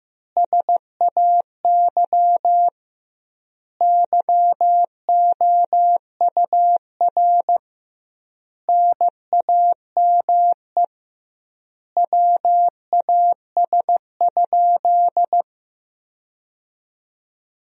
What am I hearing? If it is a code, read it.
Morse code, heard as a keyed tone, about 15 wpm, 700 Hz. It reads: SAY YOUR NAME WAS?
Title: 2x Word-Spacing - Sentences from Top 200 Words - 15wpm